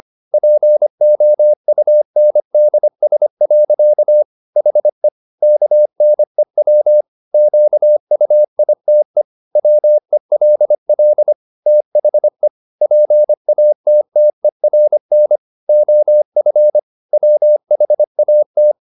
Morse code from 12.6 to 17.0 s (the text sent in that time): PATTERN OF